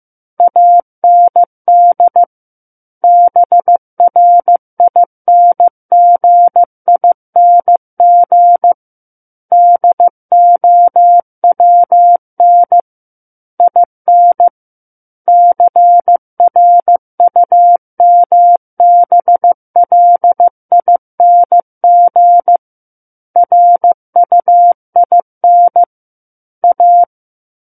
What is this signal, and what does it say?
Morse code, heard as a keyed tone, about 15 wpm, 700 Hz. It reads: AND BRINGING DOWN IN CRUMBLING RUIN A
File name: Chapter 6 - 15 WPM